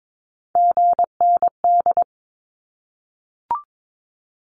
Morse code, heard as a keyed tone, about 22 wpm, 700 Hz. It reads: GND
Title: Abbreviations Repeated 3x - 22wpm